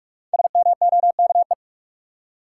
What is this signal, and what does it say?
Morse code, heard as a keyed tone, about 44 wpm, 700 Hz. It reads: SMOKE
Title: Five Letter Words - 45wpm